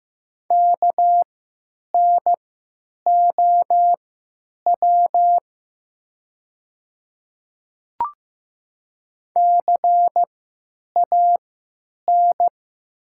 Morse code, heard as a keyed tone, about 15 wpm, spaced as for 9 wpm, 700 Hz. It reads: KNOW CAN
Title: 3x Character Spacing - Top 100 Words - 15wpm